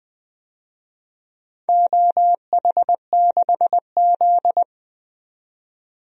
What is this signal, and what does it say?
Morse code, heard as a keyed tone, about 20 wpm, 700 Hz. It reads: OH6Z
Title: DXCC-100 Call Signs - 4 Characters - 20wpm